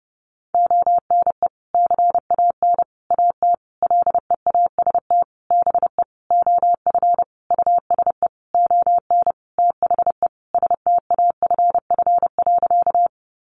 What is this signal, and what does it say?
Morse code, heard as a keyed tone, about 30 wpm, 700 Hz. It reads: ONE CAN AT LEAST BE OF USE ON THE STAFF.
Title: Sentences from Top 1200 Words 30wpm